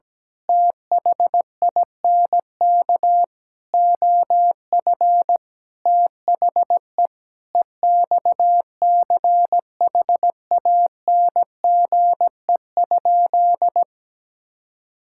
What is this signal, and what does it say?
Morse code, heard as a keyed tone, about 17 wpm, 700 Hz. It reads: THINK OF THE EXCHANGE?
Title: Sentences from Top 1000 Words - Review Part 6 - 17wpm